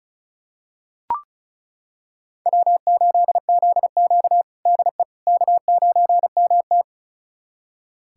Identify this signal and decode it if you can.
Morse code, heard as a keyed tone, about 36 wpm, 700 Hz. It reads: W8ZQ DE K9MT
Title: USA Callsign DE Callsign - 4 Characters - 35wpm